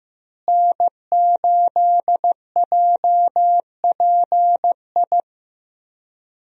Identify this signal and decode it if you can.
Morse code, heard as a keyed tone, about 15 wpm, 700 Hz. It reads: N8JPI
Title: USA Call Signs - 5 Characters - Repeated 2x - 15wpm